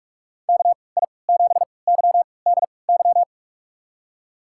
Morse code, heard as a keyed tone, about 46 wpm, spaced as for 26 wpm, 700 Hz. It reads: KI7YDY
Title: USA Call Signs - 3x Character Spacing - 45wpm